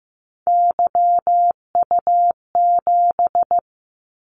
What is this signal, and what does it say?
Morse code, heard as a keyed tone, about 15 wpm, 700 Hz. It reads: YU7
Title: Callsign Prefixes for Top 25 DXCC - 15wpm